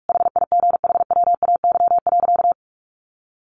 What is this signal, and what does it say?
Morse code, heard as a keyed tone, about 44 wpm, 700 Hz. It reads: HIGHWAY.
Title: Sentences from Top 1400 Words 45wpm